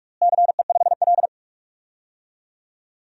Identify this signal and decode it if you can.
Morse code, heard as a keyed tone, about 41 wpm, 700 Hz. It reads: CE5L
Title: DXCC-100 Call Signs - 4 Characters - Repeated 3x - 45wpm